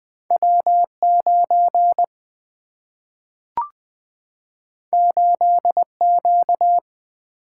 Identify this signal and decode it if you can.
Morse code, heard as a keyed tone, about 20 wpm, 700 Hz. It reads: W9 8Q
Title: Two Letter-Number Combinations - 20wpm